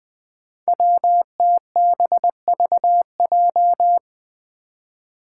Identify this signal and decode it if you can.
Morse code, heard as a keyed tone, about 20 wpm, 700 Hz. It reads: WTBVJ
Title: Five Challenging Letter-Number Combinations 20wpm